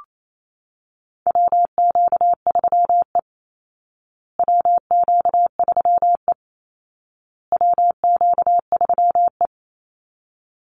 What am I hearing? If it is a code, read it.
Morse code, heard as a keyed tone, about 28 wpm, 700 Hz. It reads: WQ3E WQ3E WQ3E
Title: Call Signs Repeated 2x - 28wpm